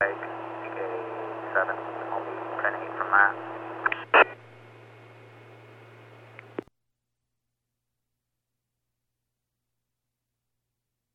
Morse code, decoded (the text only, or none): none